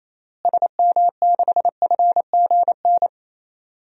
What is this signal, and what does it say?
Morse code, heard as a keyed tone, about 28 wpm, 700 Hz. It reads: SM6FGN
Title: DXCC-100 Call Signs - 6 Characters - Repeated 3x - 28wpm